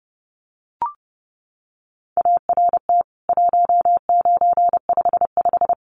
Morse code, heard as a keyed tone, about 30 wpm, 700 Hz. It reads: ART 1955
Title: CWT Exchange 30wpm